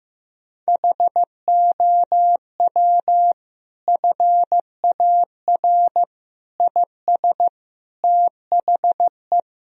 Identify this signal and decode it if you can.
Morse code, heard as a keyed tone, about 15 wpm, 700 Hz. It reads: HOW FAR IS THE
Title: Sentences from Top 1000 Words - Review Part 4 - 15wpm